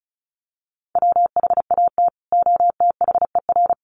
Morse code, heard as a keyed tone, about 34 wpm, 700 Hz. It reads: WHAT OTHER
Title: Sentences from Top 100 Words 35wpm